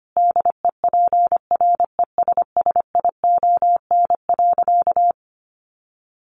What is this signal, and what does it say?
Morse code, heard as a keyed tone, about 25 wpm, 700 Hz. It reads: DEPRESSION.
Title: Sentences from Top 1300 Words 25wpm